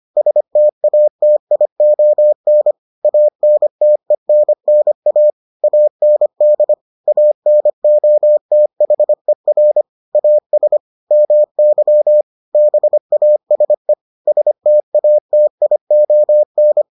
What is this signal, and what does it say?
Morse code, heard as a keyed tone, about 25 wpm, 600 Hz. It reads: STATION ANTENNA AND ANOTHER AS MY BASE STATION